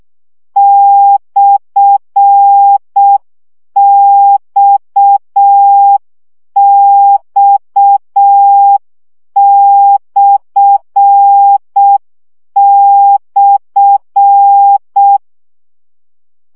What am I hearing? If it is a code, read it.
Morse code, heard as a keyed tone, about 6 wpm, 800 Hz. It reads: /XX//
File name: CW gyakorlas x per 5 baud